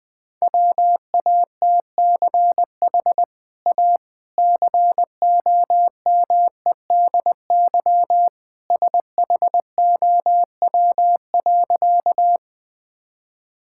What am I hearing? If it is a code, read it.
Morse code, heard as a keyed tone, about 20 wpm, 700 Hz. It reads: WATCH A COMEDY SHOW.
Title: Easy Sentences with Hints - 20wpm